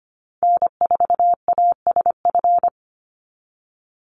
Morse code, heard as a keyed tone, about 25 wpm, 700 Hz. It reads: N4ASF
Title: Call Signs Repeated 3x - 25wpm